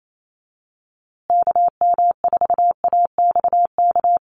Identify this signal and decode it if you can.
Morse code, heard as a keyed tone, about 28 wpm, 700 Hz. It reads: KM4AXK
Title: USA Call Signs Repeated 3x 28wpm